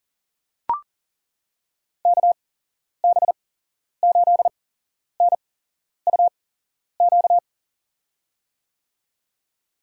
Morse code, heard as a keyed tone, about 40 wpm, spaced as for 11 wpm, 700 Hz. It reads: KB8NUQ
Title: USA Call Signs - 6 Characters - 8x Character Spacing - 40wpm